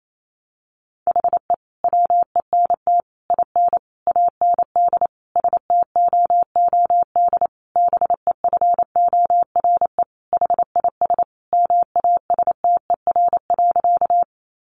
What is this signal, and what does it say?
Morse code, heard as a keyed tone, about 28 wpm, 700 Hz. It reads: HE WENT IN AND STOOD BEFORE HIS MASTER.
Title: Sentences from Top 1300 Words 28wpm